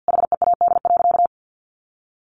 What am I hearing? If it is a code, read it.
Morse code, heard as a keyed tone, about 50 wpm, 700 Hz. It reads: HEAD.
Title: Sentences from Top 1200 Words 50wpm